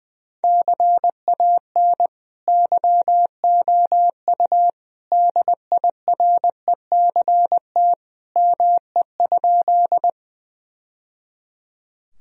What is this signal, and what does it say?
Morse code, heard as a keyed tone, about 21 wpm, 700 Hz. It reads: CAN YOU DIRECT ME?
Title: Sentences from Top 1000 Words - Review Part 2 - 20wpm